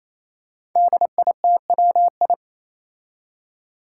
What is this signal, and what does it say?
Morse code, heard as a keyed tone, about 28 wpm, 700 Hz. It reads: DITWI
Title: Five Challenging Letter-Number Combinations 28wpm